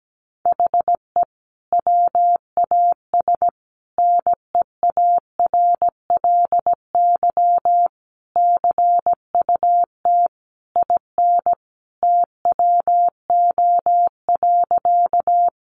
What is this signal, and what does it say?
Morse code, heard as a keyed tone, about 17 wpm, 700 Hz. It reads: HE WAS NEARLY CUT IN TWO.
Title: Sentences from Top 1300 Words 17wpm